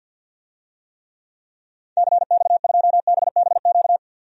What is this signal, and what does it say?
Morse code, heard as a keyed tone, about 48 wpm, 700 Hz. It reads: CX2BBQ
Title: DXCC-100 Call Signs - 6 Characters - 50wpm